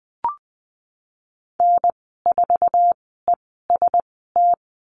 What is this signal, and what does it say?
Morse code, heard as a keyed tone, about 20 wpm, spaced as for 14 wpm, 700 Hz. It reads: N4EST
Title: 048 Lesson 9g - Number 4 - Call Signs 20wpm